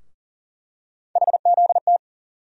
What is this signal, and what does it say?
Morse code, heard as a keyed tone, about 40 wpm, 700 Hz. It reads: HZT
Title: USA Call Sign Postfixes - 3 Letters 40wpm